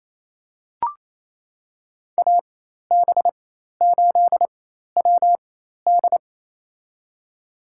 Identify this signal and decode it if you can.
Morse code, heard as a keyed tone, about 27 wpm, spaced as for 13 wpm, 700 Hz. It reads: AB8WD
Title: USA Call Signs - 5 Characters - 4x Character Spacing - 28wpm